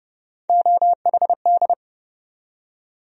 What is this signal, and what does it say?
Morse code, heard as a keyed tone, about 30 wpm, 700 Hz. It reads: OHD
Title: USA Call Sign Postfixes - 3 Letters 30wpm